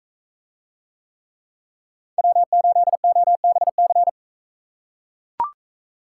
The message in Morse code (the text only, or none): W8OBC